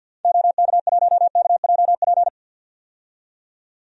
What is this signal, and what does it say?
Morse code, heard as a keyed tone, about 51 wpm, 700 Hz. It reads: OK1KJP
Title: DXCC-100 Call Signs - 6 Characters - 50wpm